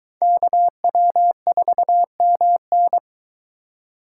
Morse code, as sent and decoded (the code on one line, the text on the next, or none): -.- .-- ....- -- -.
KW4MN